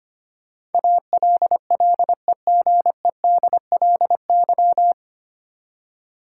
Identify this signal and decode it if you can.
Morse code, heard as a keyed tone, about 25 wpm, 700 Hz. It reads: ALLEGEDLY
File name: Nine Letter Words - 25wpm